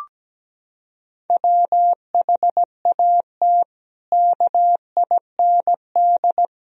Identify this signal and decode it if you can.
Morse code, heard as a keyed tone, about 17 wpm, 700 Hz. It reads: WHAT KIND
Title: Sentences from Top 1000 Words - Review Part 4 - 17wpm